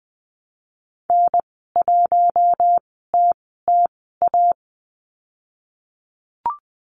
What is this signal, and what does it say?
Morse code, heard as a keyed tone, about 20 wpm, spaced as for 15 wpm, 700 Hz. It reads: N1TTA
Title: 088 Lesson 15g - Number 5 - Call Signs 20wpm